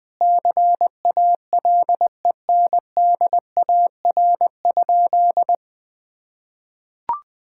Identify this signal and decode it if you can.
Morse code, heard as a keyed tone, about 20 wpm, 700 Hz. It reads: CALENDAR?
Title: Sentences from Top 1600 Words 20wpm